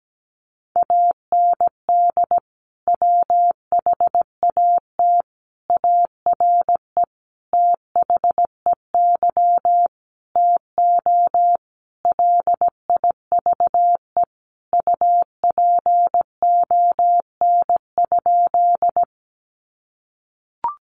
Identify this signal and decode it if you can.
Morse code, heard as a keyed tone, about 17 wpm, 700 Hz. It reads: AND WHAT ARE THEY TO LIVE UPON?
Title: Sentences from Top 1100 Words 17wpm